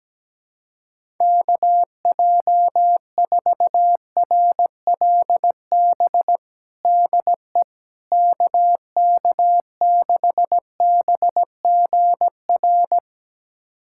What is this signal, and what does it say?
Morse code, heard as a keyed tone, about 17 wpm, 700 Hz. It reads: KJ4RLB DE KK6BGR
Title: USA Callsign DE Callsign - 6 Characters - 17wpm